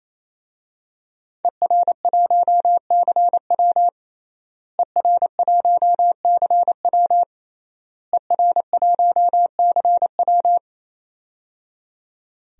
Morse code, heard as a keyed tone, about 28 wpm, 700 Hz. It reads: ER1CW ER1CW ER1CW
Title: DXCC-100 Call Signs - 5 Characters - Repeated 3x - 28wpm